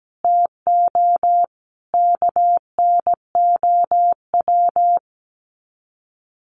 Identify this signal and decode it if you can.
Morse code, heard as a keyed tone, about 17 wpm, 700 Hz. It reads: TO KNOW
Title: Sets of 5 Words Encore 17wpm